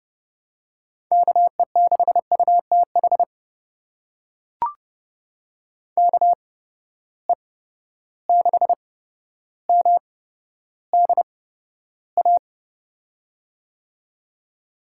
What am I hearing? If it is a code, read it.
Morse code, heard as a keyed tone, about 30 wpm, spaced as for 13 wpm, 700 Hz. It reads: KE6UTH KE6MDA
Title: USA Call Signs - 6 Characters - 8x Character Spacing - 30wpm